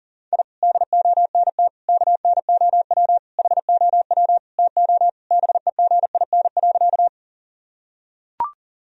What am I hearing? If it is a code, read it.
Morse code, heard as a keyed tone, about 40 wpm, 700 Hz. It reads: I DONT KNOW HOW TO BEGIN.
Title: Sentences from Top 1000 Words - Review Part 1 - 40wpm